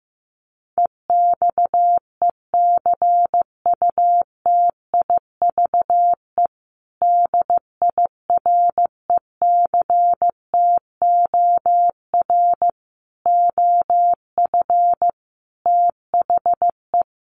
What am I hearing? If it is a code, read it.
Morse code, heard as a keyed tone, about 15 wpm, 700 Hz. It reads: EXECUTIVE DIRECTOR OF THE
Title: Sets of 5 Words Encore2 15wpm